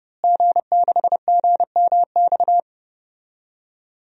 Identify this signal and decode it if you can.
Morse code, heard as a keyed tone, about 30 wpm, 700 Hz. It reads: G6GMX
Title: DXCC-100 Call Signs - 5 Characters - Repeated 3x - 30wpm